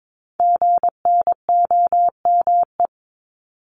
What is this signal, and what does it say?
Morse code, heard as a keyed tone, about 22 wpm, 700 Hz. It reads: GNOME
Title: Multi-letter Phonemes 22wpm